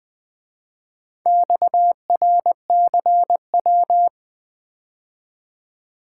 Morse code, heard as a keyed tone, about 24 wpm, 700 Hz. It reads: XRCW
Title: Speed-Racing - Four Letter-Number Combinations - 20wpm